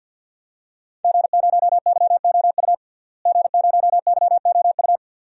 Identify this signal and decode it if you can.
Morse code, heard as a keyed tone, about 51 wpm, 700 Hz. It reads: G0YOU G0YOU
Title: DXCC-100 Call Signs - 5 Characters - Repeated 2x - 50wpm